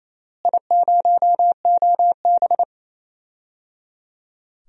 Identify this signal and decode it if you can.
Morse code, heard as a keyed tone, about 28 wpm, 700 Hz. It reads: I0OB
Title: DXCC-100 Call Signs - 4 Characters - Repeated 3x - 28wpm